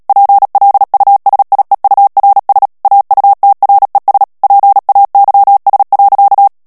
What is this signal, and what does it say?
Morse code, heard as a keyed tone, about 37 wpm, 800 Hz. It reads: PLUSIEURS AUTRES PAYS.
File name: CW / FICHIERS-MP3 / FRANCAIS TEXTE CLAIR 1 VITESSE 35 MOTS MINUTES